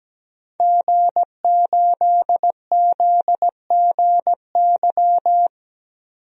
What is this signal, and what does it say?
Morse code, heard as a keyed tone, about 17 wpm, 700 Hz. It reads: G8ZGY
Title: DXCC-100 Call Signs - 5 Characters - 17wpm